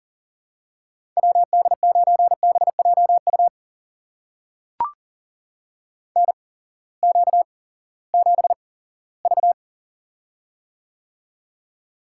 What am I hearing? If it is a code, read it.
Morse code, heard as a keyed tone, about 40 wpm, spaced as for 20 wpm, 700 Hz. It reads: WD9BJU NQ7V
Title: USA Call Signs - 8x Character Spacing - 40wpm